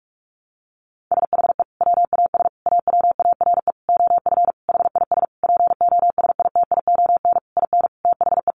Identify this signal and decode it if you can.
Morse code, heard as a keyed tone, about 44 wpm, 700 Hz. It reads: SHE WAS AWARE OF HIS POSITION IN THE
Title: Sentences from Top 1200 Words 45wpm